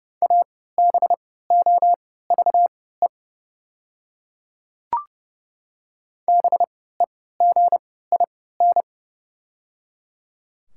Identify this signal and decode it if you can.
Morse code, heard as a keyed tone, about 29 wpm, spaced as for 17 wpm, 690 Hz. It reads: ABOVE BEGIN
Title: 3x Character Spacing - Top 500 Words - 30wpm